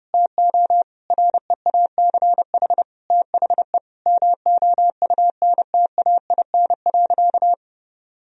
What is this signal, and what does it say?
Morse code, heard as a keyed tone, about 30 wpm, 700 Hz. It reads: TO REACH THE MOUNTAIN.
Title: Sentences from Top 1000 Words - Review Part 2 - 30wpm